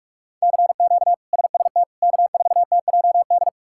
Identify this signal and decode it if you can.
Morse code, heard as a keyed tone, about 47 wpm, 700 Hz. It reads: CQ SST K4TJD
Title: K1USN SST QSO Practice - 45wpm